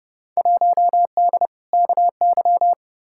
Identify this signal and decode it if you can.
Morse code, heard as a keyed tone, about 29 wpm, 700 Hz. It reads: 1D KY
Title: Field Day Exchanges - 30wpm